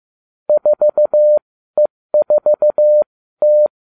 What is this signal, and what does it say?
Morse code, heard as a keyed tone, about 15 wpm, spaced as for 13 wpm, 600 Hz. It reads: 4E4T